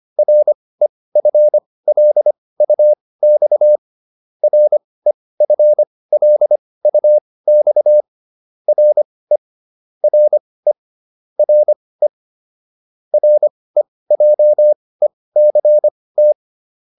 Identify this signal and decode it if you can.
Morse code, heard as a keyed tone, about 25 wpm, spaced as for 18 wpm, 600 Hz. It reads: REFLUX REFLUX RE RE RE REJECT